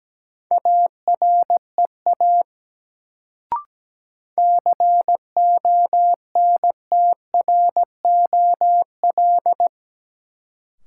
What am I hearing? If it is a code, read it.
Morse code, heard as a keyed tone, about 17 wpm, 700 Hz. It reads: AREA CONTROL
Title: Sentences from Top 1000 Words - Review Part 2 - New Word Drill - 17wpm